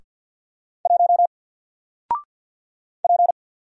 Morse code, heard as a keyed tone, about 49 wpm, 700 Hz. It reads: 1 P
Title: Single Letter-Number Alt 50wpm